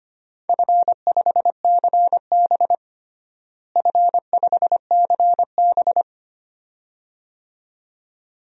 Morse code, heard as a keyed tone, about 25 wpm, 700 Hz. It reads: F5CB F5CB
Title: DXCC-100 Call Signs - 4 Characters - Repeated 2x - 25wpm